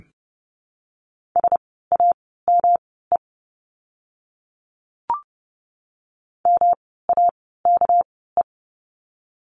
Morse code, heard as a keyed tone, about 30 wpm, spaced as for 17 wpm, 700 Hz. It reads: SAME MAKE
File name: course / 30:10wpm / 237 Lesson 38f - Letter q - Words 30wpm